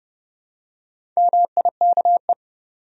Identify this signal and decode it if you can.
Morse code, heard as a keyed tone, about 30 wpm, 700 Hz. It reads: MIKE
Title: USA Common Names - Max 5 Letters - 30wpm